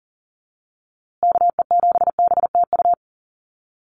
Morse code, heard as a keyed tone, about 40 wpm, 700 Hz. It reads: KE7BTU
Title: USA Call Signs Repeated 2x 40wpm